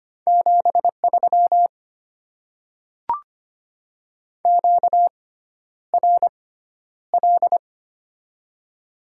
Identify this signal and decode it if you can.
Morse code, heard as a keyed tone, about 25 wpm, spaced as for 11 wpm, 700 Hz. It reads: 73 QRL
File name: Top 20 QSO Elements - 6x Character Spacing - 25wpm